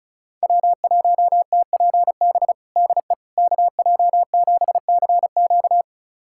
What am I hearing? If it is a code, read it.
Morse code, heard as a keyed tone, about 34 wpm, 700 Hz. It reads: W1TPB DE KJ7CQ
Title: USA Callsign DE Callsign - 5 Characters - 35wpm